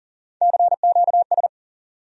Morse code, heard as a keyed tone, about 40 wpm, 700 Hz. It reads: CQS
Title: USA Call Sign Postfixes - 3 Letters 40wpm